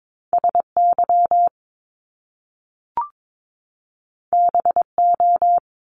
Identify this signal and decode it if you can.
Morse code, heard as a keyed tone, about 22 wpm, 700 Hz. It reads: SY BO
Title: Top Two Letter Combinations in Words 22wpm